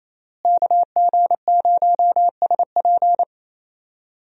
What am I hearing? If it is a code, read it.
Morse code, heard as a keyed tone, about 28 wpm, 700 Hz. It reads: KG0SP
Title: USA Call Signs - 5 Characters - Repeated 3x - 28wpm